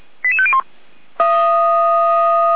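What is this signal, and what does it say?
Five-tone selective call (ZVEI), heard as quick runs of five tones, tones of about 70 ms. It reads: 88501
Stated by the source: ZVEI five-tone call received by SDR, fire brigade feuerwehr-orlamuende.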